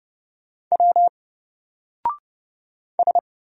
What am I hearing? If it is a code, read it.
Morse code, heard as a keyed tone, about 30 wpm, 700 Hz. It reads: W S
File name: Single-Letter-Number-30wpm